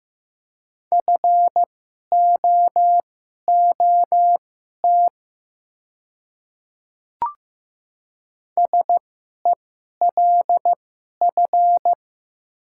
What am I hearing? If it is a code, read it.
Morse code, heard as a keyed tone, about 15 wpm, spaced as for 11 wpm, 700 Hz. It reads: FOOT SELF
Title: 2x Character Spacing - Top 500 Words - 15wpm